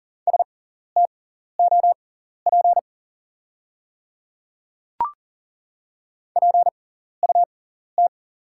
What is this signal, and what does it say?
Morse code, heard as a keyed tone, about 39 wpm, spaced as for 14 wpm, 700 Hz. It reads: STOP PUT
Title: 6x Character Spacing - Top 500 Words - 40wpm